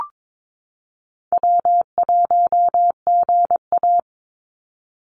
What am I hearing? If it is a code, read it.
Morse code, heard as a keyed tone, about 22 wpm, 700 Hz. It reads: W1GA
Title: USA Call Signs 22wpm